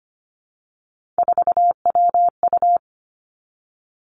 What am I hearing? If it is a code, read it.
Morse code, heard as a keyed tone, about 25 wpm, 700 Hz. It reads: 4WU
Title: Three Challenging Letter-Number Combinations 25wpm